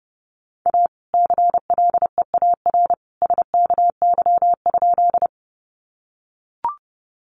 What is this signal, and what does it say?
Morse code, heard as a keyed tone, about 30 wpm, 700 Hz. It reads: A CLEAR SKY?
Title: Sentences from Top 1200 Words 30wpm